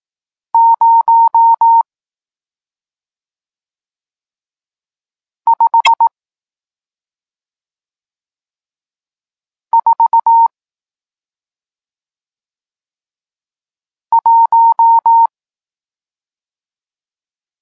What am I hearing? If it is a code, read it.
Morse code, heard as a keyed tone, about 18 wpm, spaced as for 2 wpm, 920 Hz. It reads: 0541
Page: Tekst 40 – Nauka znaków przestankowych